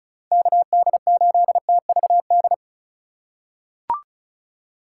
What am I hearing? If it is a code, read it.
Morse code, heard as a keyed tone, about 34 wpm, 700 Hz. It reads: KD8TVD
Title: USA Call Signs - 6 Characters - 2x Character Spacing - 35wpm